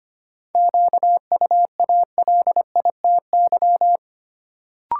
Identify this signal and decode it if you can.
Morse code, heard as a keyed tone, about 25 wpm, 700 Hz. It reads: QUALITY
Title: Sentences from Top 1000 Words - Review Part 4 - New Word Drill - 25wpm